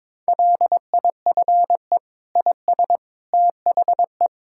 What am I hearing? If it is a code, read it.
Morse code, heard as a keyed tone, about 21 wpm, 700 Hz. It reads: LIFE IS THE
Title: Sentences from Top 1000 Words - Review Part 6 - 22wpm